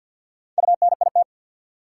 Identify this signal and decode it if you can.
Morse code, heard as a keyed tone, about 47 wpm, 700 Hz. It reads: UNIT
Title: Sentences from Top 1000 Words - Review Part 2 - New Word Drill - 50wpm